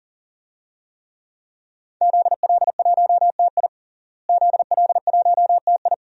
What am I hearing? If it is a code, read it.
Morse code, heard as a keyed tone, about 38 wpm, 700 Hz. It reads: ZL1TI ZL1TI
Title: DXCC-100 Call Signs - 5 Characters - Repeated 2x - 40wpm